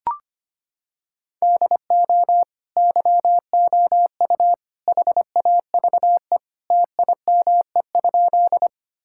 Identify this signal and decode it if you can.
Morse code, heard as a keyed tone, about 25 wpm, 700 Hz. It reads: DO YOU HAVE TIME?
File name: practice / 25wpm / Sentences with 3-Word Phrases - 25wpm